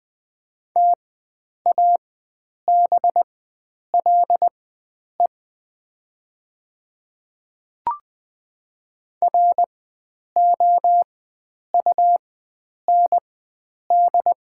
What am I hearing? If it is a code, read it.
Morse code, heard as a keyed tone, about 20 wpm, spaced as for 9 wpm, 700 Hz. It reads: TABLE ROUND
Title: 4x Character Spacing - Top 500 Words - 20wpm